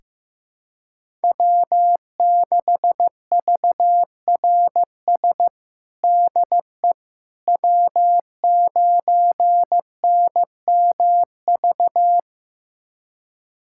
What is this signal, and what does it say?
Morse code, heard as a keyed tone, about 16 wpm, 700 Hz. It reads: W6VRS DE W9NMV
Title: USA Callsign DE Callsign - 5 Characters - 15wpm